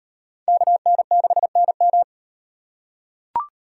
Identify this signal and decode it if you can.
Morse code, heard as a keyed tone, about 37 wpm, 700 Hz. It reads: KN6NM